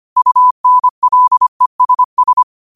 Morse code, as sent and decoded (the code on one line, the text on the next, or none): ..- -. .-.. ... ...
UNLESS